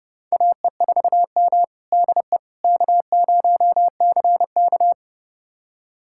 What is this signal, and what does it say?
Morse code, heard as a keyed tone, about 30 wpm, 700 Hz. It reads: AE4M DE K0CK
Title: USA Callsign DE Callsign - 4 Characters - 30wpm